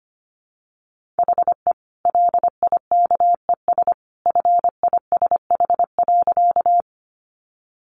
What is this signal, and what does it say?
Morse code, heard as a keyed tone, about 25 wpm, 700 Hz. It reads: HE LIKES FISH.
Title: Additional Easy Phrases 25wpm